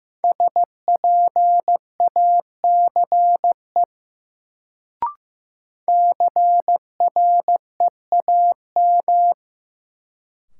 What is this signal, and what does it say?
Morse code, heard as a keyed tone, about 15 wpm, 700 Hz. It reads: SPACE CREAM
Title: Sentences from Top 1500 Words - New Word Drill Alt - 15wpm